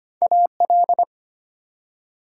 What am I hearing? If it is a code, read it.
Morse code, heard as a keyed tone, about 25 wpm, 700 Hz. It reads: AL
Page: morse-code-ninja-practice-25wpm